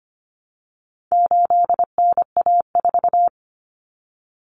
Morse code, heard as a keyed tone, about 25 wpm, 700 Hz. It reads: 8NA4
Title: Common Cut Numbers - 4-Digit 25wpm